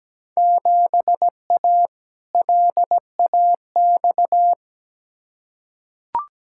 Morse code, heard as a keyed tone, about 17 wpm, 700 Hz. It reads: 7A LAX
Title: Field Day Exchanges - 17wpm